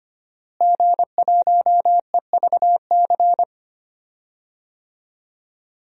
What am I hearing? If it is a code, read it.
Morse code, heard as a keyed tone, about 25 wpm, 700 Hz. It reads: G1EVC
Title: DXCC-100 Call Signs - 5 Characters - Repeated 3x - 25wpm